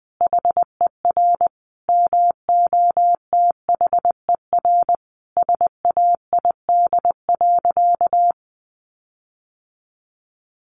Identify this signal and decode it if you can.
Morse code, heard as a keyed tone, about 20 wpm, 700 Hz. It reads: HER MOTHER SAID.
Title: Sentences from Top 1500 Words 20wpm